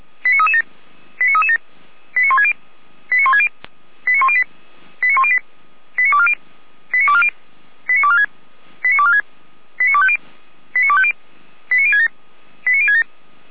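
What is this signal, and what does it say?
Five-tone selective call (ZVEI), heard as quick runs of five tones, tones of about 70 ms. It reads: 89337 89337 89160 89160 89108 89108 89240 89240 89246 89246 89250 89250 89067 89067
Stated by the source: ZVEI five-tone selective call 89075 received by SDR.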